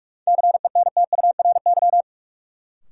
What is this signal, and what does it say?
Morse code, heard as a keyed tone, about 46 wpm, 700 Hz. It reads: CENTURY
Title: Sets of 5 Words - New Word Drill - 45wpm